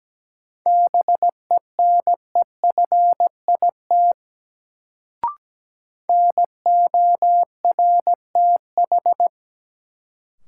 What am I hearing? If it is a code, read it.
Morse code, heard as a keyed tone, about 17 wpm, 700 Hz. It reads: BENEFIT NORTH
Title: Sets of 5 Words Encore - New Word Drill - 17wpm